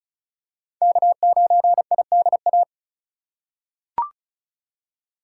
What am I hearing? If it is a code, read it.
Morse code, heard as a keyed tone, about 34 wpm, 700 Hz. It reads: K9IDA
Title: USA Call Signs - 5 Characters - 4x Character Spacing - 35wpm